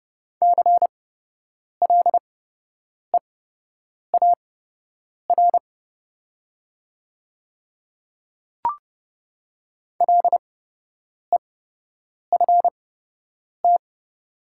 Morse code, heard as a keyed tone, about 30 wpm, spaced as for 8 wpm, 700 Hz. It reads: CLEAR LEFT
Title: 8x Character Spacing - Top 500 Words - 30wpm